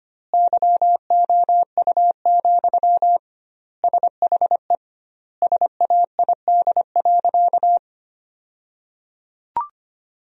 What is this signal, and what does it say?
Morse code, heard as a keyed tone, about 25 wpm, 700 Hz. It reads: YOU, SHE SAID.
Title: 2x Word-Spacing - Sentences from Top 400 Words - 25wpm